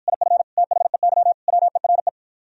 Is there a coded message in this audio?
Yes: Morse code from 0.1 to 2.1 s, heard as a keyed tone, about 53 wpm, 700 Hz. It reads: IF THEY WERE